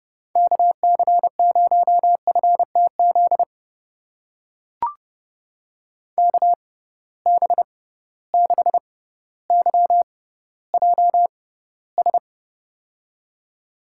Morse code, heard as a keyed tone, about 30 wpm, spaced as for 15 wpm, 700 Hz. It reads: KC0FTZ KB6YJS
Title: USA Call Signs - 6 Characters - 6x Character Spacing - 30wpm